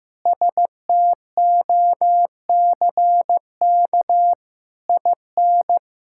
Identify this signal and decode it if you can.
Morse code, heard as a keyed tone, about 15 wpm, 700 Hz. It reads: STOCK IN
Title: Sentences from Top 1000 Words - Review Part 5 - 15wpm